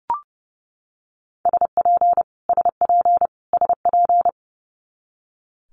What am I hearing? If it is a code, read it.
Morse code, heard as a keyed tone, about 29 wpm, 700 Hz. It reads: SP SP SP
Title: Multi-letter Phonemes Repeated 3x 30wpm